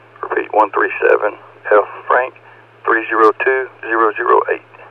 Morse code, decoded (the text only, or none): none